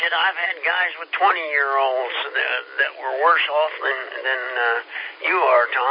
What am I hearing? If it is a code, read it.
Emergency Alert System (SAME) tones (header bursts), heard as harsh buzzing bursts, with speech. It is heard from 0.3 to 5.1 s.